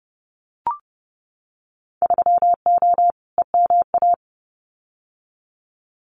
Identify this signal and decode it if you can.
Morse code, heard as a keyed tone, about 30 wpm, 700 Hz. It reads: 3O EMA
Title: Winter Field Day Exchanges 30wpm